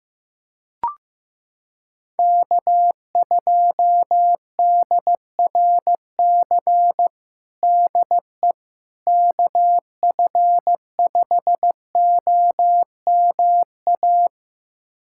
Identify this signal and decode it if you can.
Morse code, heard as a keyed tone, about 16 wpm, 700 Hz. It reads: K2DRC DE KF5OMA
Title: USA Callsign DE Callsign 15wpm